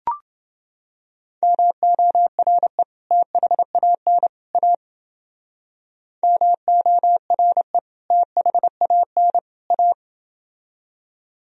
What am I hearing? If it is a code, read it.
Morse code, heard as a keyed tone, about 30 wpm, 700 Hz. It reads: MORE THAN A MORE THAN A